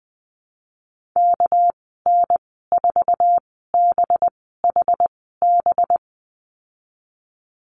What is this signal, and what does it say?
Morse code, heard as a keyed tone, about 20 wpm, spaced as for 15 wpm, 700 Hz. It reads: KN4BHB
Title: 238 Lesson 38g - Letter q - Call Signs 20wpm